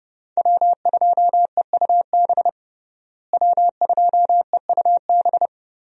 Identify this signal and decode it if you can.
Morse code, heard as a keyed tone, about 31 wpm, 700 Hz. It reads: W2EUB W2EUB
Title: USA Call Signs - 5 Characters - Repeated 2x - 30wpm